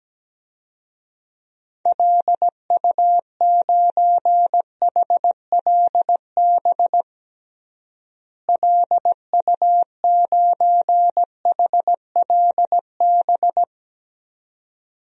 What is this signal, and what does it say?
Morse code, heard as a keyed tone, about 17 wpm, 700 Hz. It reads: LU9HLB LU9HLB
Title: DXCC-100 Call Signs - 6 Characters - Repeated 2x - 17wpm